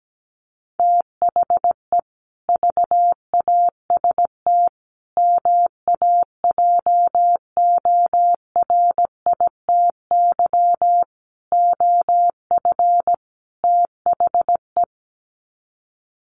Morse code, heard as a keyed tone, about 17 wpm, 700 Hz. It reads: THE VAST MAJORITY OF THE
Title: Sets of 5 Words Encore 17wpm